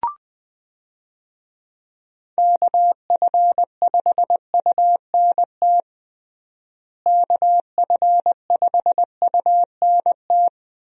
Morse code, heard as a keyed tone, about 21 wpm, 700 Hz. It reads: KF5UNT KF5UNT
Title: USA Call Signs - 6 Characters - Repeated 3x - 20wpm